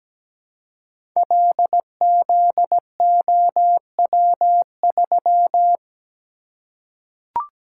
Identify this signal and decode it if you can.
Morse code, heard as a keyed tone, about 17 wpm, 700 Hz. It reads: LZOW3
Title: Code Groups - 5 Letter:Numbers - 17wpm